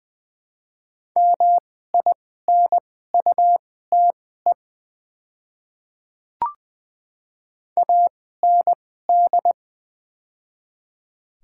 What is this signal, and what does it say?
Morse code, heard as a keyed tone, about 20 wpm, spaced as for 15 wpm, 700 Hz. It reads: MINUTE AND
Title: 2x Character Spacing - Top 500 Words - 20wpm